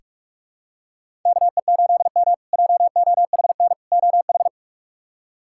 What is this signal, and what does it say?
Morse code, heard as a keyed tone, about 46 wpm, 700 Hz. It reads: KE8M JOHN OH
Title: K1USN SST QSO Practice - 45wpm